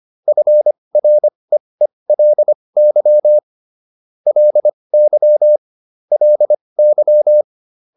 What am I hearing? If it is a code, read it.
Morse code, heard as a keyed tone, about 25 wpm, spaced as for 20 wpm, 600 Hz. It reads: FREELY LY LY